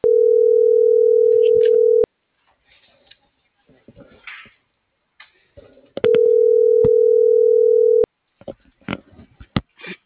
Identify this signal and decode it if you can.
Ringback tone (North America), 440 + 480 Hz.